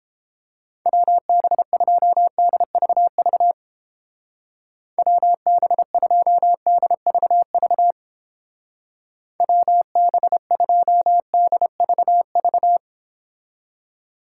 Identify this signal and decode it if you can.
Morse code, heard as a keyed tone, about 31 wpm, 700 Hz. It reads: WB2DVV WB2DVV WB2DVV